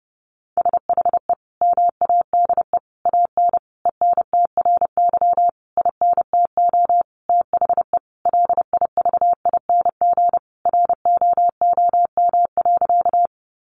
Morse code, heard as a keyed tone, about 30 wpm, 700 Hz. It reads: SHE MADE AN ENTRY INTO THE LIVING ROOM.
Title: Sentences from Top 1500 Words 30wpm